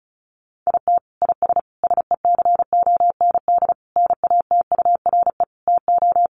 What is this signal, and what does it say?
Morse code, heard as a keyed tone, about 34 wpm, 700 Hz. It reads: IT IS SECOND NATURE TO
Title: Sentences from Top 600 Words 35wpm